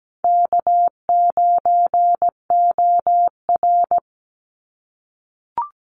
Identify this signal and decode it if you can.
Morse code, heard as a keyed tone, about 17 wpm, 700 Hz. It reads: K9OR
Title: Easy Call Signs - 17wpm